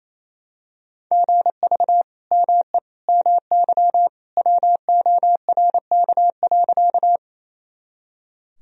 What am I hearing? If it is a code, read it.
Morse code, heard as a keyed tone, about 27 wpm, 700 Hz. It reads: GV ME MY WORK.
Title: Sentences from Top 200 Words and Abbreviations 28wpm